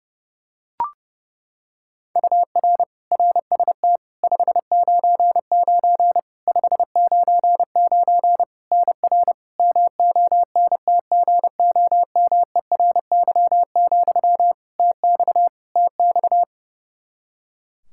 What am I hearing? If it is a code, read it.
Morse code, heard as a keyed tone, about 30 wpm, 700 Hz. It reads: UR RST 599 599 NR MONTGOMERY, TX TX
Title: Practice QSO - Exchange 1 - Beginning - 30wpm